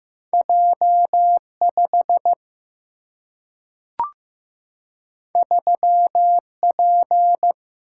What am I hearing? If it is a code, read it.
Morse code, heard as a keyed tone, about 15 wpm, 700 Hz. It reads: J 5 3 P
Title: Single-Letter Single-Number 15wpm